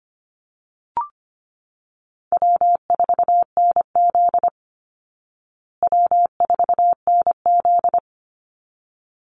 Morse code, heard as a keyed tone, about 25 wpm, 700 Hz. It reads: W4NZ W4NZ
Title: Easy Call Signs Repeated 2x - 25wpm